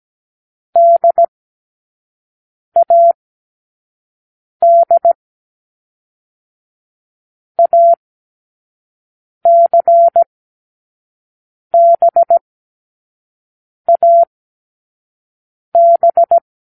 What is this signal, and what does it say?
Morse code, heard as a keyed tone, about 17 wpm, 680 Hz. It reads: D A D A C B A B
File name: k7qo / Lesson6(A-D)